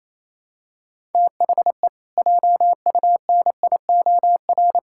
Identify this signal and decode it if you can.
Morse code, heard as a keyed tone, about 28 wpm, 700 Hz. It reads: THE JUNIOR
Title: Pangram Challenge v2 - 28wpm